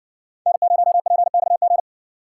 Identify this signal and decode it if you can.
Morse code, heard as a keyed tone, about 57 wpm, 700 Hz. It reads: N0PXG